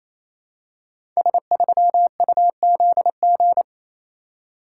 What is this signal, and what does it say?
Morse code, heard as a keyed tone, about 29 wpm, 700 Hz. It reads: S3UZG
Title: Code Groups - 5 Letter:Numbers - 28wpm